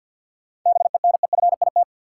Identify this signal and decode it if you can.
Morse code, heard as a keyed tone, about 51 wpm, 700 Hz. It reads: BENEFIT
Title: Sets of 5 Words Encore - New Word Drill - 50wpm